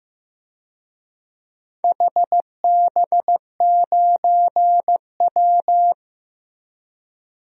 Morse code, heard as a keyed tone, about 15 wpm, 700 Hz. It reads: HB9W
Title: DXCC-100 Call Signs - 4 Characters - Repeated 2x - 15wpm